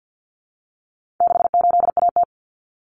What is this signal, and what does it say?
Morse code, heard as a keyed tone, about 50 wpm, 700 Hz. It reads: 68AT
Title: Common Cut Numbers - 4-Digit 50wpm